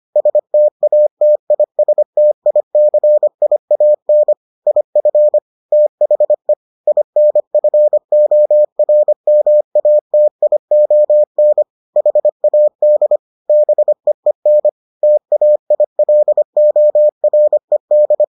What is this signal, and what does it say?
Morse code, heard as a keyed tone, about 25 wpm, 600 Hz. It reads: STATISTICIAN IF THE INFORMATION HAD BEEN TAILORED